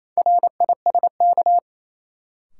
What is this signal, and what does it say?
Morse code, heard as a keyed tone, about 28 wpm, 700 Hz. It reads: RISK
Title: Sentences from Top 600 Words - New Word Drill - 28wpm